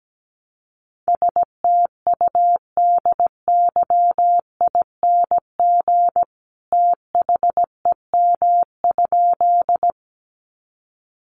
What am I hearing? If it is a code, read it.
Morse code, heard as a keyed tone, about 17 wpm, 700 Hz. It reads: STUDYING THEM?
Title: A Sound Of Thunder 17wpm